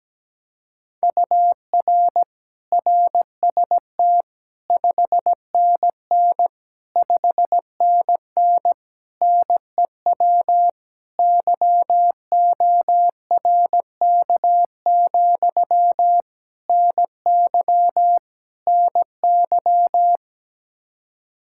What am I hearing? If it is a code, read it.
Morse code, heard as a keyed tone, about 17 wpm, 700 Hz. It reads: UR RST 5NN 5NN NEW YORK, NY NY
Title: Practice QSO - Exchange 1 - Beginning - 17wpm